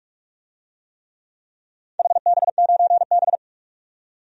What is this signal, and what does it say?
Morse code, heard as a keyed tone, about 46 wpm, 700 Hz. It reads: HB9B
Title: DXCC-100 Call Signs - 4 Characters - 45wpm